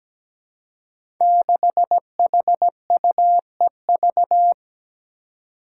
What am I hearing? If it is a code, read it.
Morse code, heard as a keyed tone, about 17 wpm, 700 Hz. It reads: 6HUEV
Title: Five Challenging Letter-Number Combinations 17wpm